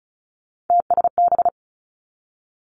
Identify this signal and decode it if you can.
Morse code, heard as a keyed tone, about 36 wpm, 700 Hz. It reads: TSB
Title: Three Challenging Letter-Number Combinations 35wpm